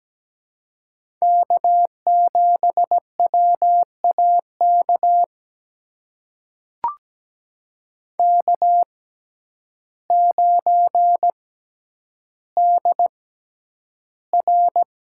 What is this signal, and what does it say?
Morse code, heard as a keyed tone, about 17 wpm, spaced as for 9 wpm, 700 Hz. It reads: K7WAK K9DR
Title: USA Call Signs - 5 Characters - 6x Character Spacing - 17wpm